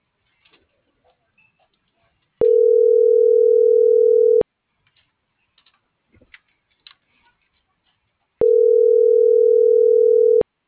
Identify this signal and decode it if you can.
Ringback tone (North America), 440 + 480 Hz.
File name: CENTRAL MULTIMODAL / out josimar.cruz - .69925